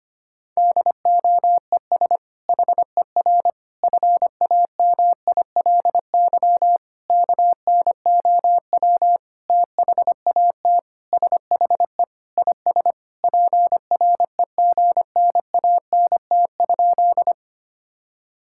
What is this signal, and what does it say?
Morse code, heard as a keyed tone, about 25 wpm, 700 Hz. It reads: DOES HER FAMILY KNOW THAT SHE IS PREGNANT?
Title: Sentences from Top 1000 Words - Review Part 6 - 25wpm